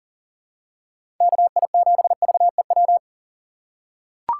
Morse code, heard as a keyed tone, about 40 wpm, 700 Hz. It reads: KI7VEW
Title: USA Call Signs - 6 Characters - 4x Character Spacing - 40wpm